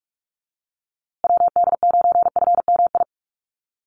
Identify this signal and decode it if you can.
Morse code, heard as a keyed tone, about 44 wpm, 700 Hz. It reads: WD9FMI
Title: USA Call Signs Repeated 3x 45wpm